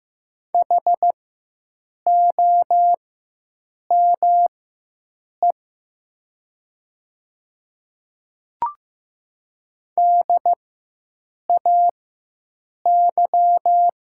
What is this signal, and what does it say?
Morse code, heard as a keyed tone, about 15 wpm, spaced as for 7 wpm, 700 Hz. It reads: HOME DAY
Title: 4x Character Spacing - Top 300 Words - 15wpm